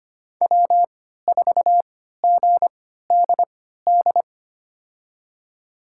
Morse code, heard as a keyed tone, about 25 wpm, spaced as for 14 wpm, 700 Hz. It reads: W4GDD
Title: USA Call Signs - 5 Characters - 3x Character Spacing - 25wpm